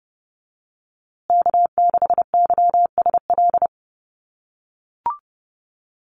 Morse code, heard as a keyed tone, about 30 wpm, 700 Hz. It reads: K6YSL
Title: USA Call Signs Repeated 2x 30wpm